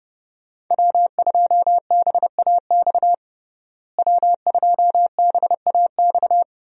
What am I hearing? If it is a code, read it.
Morse code, heard as a keyed tone, about 30 wpm, 700 Hz. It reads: W2BAX W2BAX
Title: USA Call Signs - 5 Characters - Repeated 2x - 30wpm